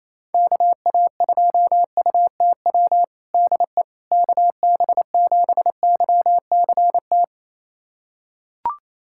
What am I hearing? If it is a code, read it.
Morse code, heard as a keyed tone, about 28 wpm, 700 Hz. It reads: KA2UTW DE KB7YCT
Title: USA Callsign DE Callsign 28wpm